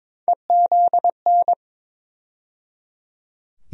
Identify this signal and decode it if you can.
Morse code, heard as a keyed tone, about 22 wpm, 700 Hz. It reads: EZN
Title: USA Call Sign Postfixes - 3 Letters 22wpm